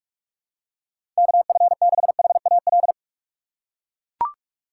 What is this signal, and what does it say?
Morse code, heard as a keyed tone, about 42 wpm, 700 Hz. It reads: KF6HAL
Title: USA Call Signs - 6 Characters - 2x Character Spacing - 45wpm